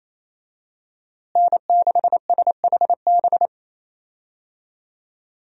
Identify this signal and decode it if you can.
Morse code, heard as a keyed tone, about 28 wpm, 700 Hz. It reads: N6SHB
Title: USA Call Signs - 5 Characters - Repeated 2x - 28wpm